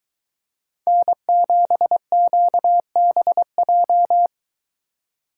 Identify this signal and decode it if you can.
Morse code, heard as a keyed tone, about 21 wpm, 700 Hz. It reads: N7QBJ